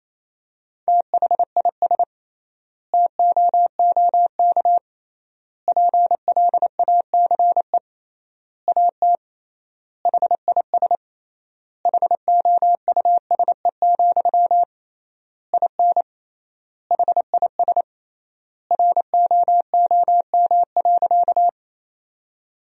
Morse code, heard as a keyed tone, about 28 wpm, 700 Hz. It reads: THIS TOOK PLACE AT HIS HOUSE, IN HIS ROOM.
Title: 3x Word-Spacing - Sentences from Top 300 Words - 28wpm